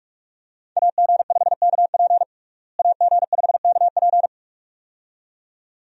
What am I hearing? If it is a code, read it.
Morse code, heard as a keyed tone, about 46 wpm, 700 Hz. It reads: AG5KP AG5KP
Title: USA Call Signs - 5 Characters - Repeated 2x - 45wpm